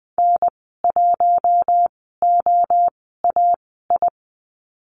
Morse code, heard as a keyed tone, about 20 wpm, spaced as for 15 wpm, 700 Hz. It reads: N1OAI
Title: 042 Lesson 8g - Number 1 - Call Signs 20wpm